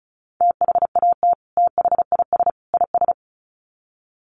Morse code, heard as a keyed tone, about 34 wpm, 700 Hz. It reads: THAT THIS IS
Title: Sets of 3 Words 35wpm